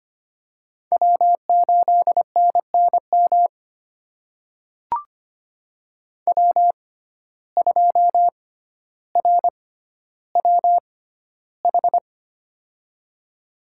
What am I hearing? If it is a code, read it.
Morse code, heard as a keyed tone, about 25 wpm, spaced as for 13 wpm, 700 Hz. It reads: W8NNM W2RWH
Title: USA Call Signs - 6x Character Spacing - 25wpm